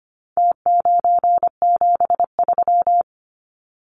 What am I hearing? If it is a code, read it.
Morse code, heard as a keyed tone, about 25 wpm, 700 Hz. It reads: T973
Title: Numbers - Contest Style - 4-Digit 25wpm